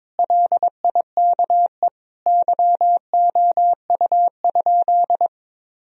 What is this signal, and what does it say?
Morse code, heard as a keyed tone, about 22 wpm, 700 Hz. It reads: LIKE YOU?
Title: Sentences from Top 1000 Words - Review Part 1 - 22wpm